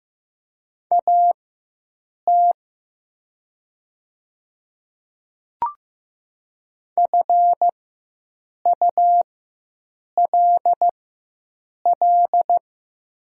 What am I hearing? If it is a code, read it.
Morse code, heard as a keyed tone, about 15 wpm, spaced as for 7 wpm, 690 Hz. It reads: AT FULL
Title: 4x Character Spacing - Top 500 Words - 15wpm